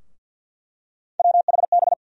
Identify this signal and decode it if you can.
Morse code, heard as a keyed tone, about 50 wpm, 700 Hz. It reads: WHB
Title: USA Call Sign Postfixes - 3 Letters 50wpm